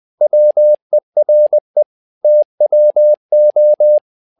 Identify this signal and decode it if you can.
Morse code, heard as a keyed tone, about 20 wpm, 600 Hz. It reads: WERE TWO
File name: tower trouble 20wpm